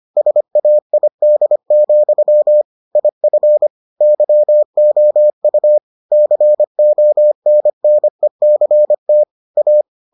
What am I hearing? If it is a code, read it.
Morse code, heard as a keyed tone, about 25 wpm, 600 Hz. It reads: SAID, IF YOU CONNECT A